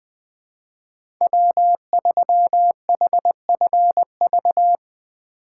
Morse code, heard as a keyed tone, about 20 wpm, 700 Hz. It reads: W3HFV
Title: USA Call Signs - 5 Characters - 2x Character Spacing - 20wpm